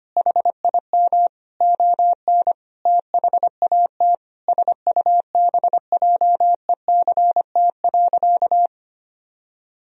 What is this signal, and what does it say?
Morse code, heard as a keyed tone, about 25 wpm, 700 Hz. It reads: HIM ON THAT SUBJECT.
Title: Sentences from Top 1000 Words - Review Part 4 - 25wpm